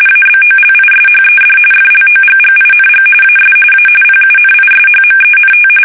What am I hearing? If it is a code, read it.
RTTY, about 45.45 baud. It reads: ABOLISHED FLOGGING AS A PUNISHMENT
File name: fsk850 / usnavy2